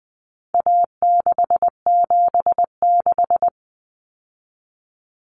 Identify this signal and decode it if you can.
Morse code, heard as a keyed tone, about 20 wpm, 700 Hz. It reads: A676
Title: Common Cut Numbers - 4-Digit 20wpm